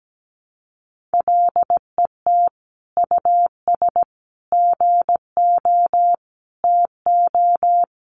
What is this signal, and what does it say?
Morse code, heard as a keyed tone, about 17 wpm, 700 Hz. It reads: LET US GO TO
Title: Sentences from Top 1500 Words 17wpm